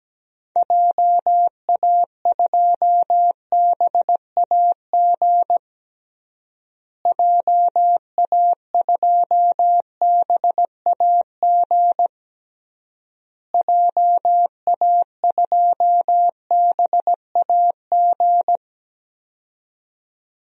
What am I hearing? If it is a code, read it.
Morse code, heard as a keyed tone, about 17 wpm, 700 Hz. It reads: JA2BAG JA2BAG JA2BAG
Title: DXCC-100 Call Signs - 6 Characters - Repeated 3x - 17wpm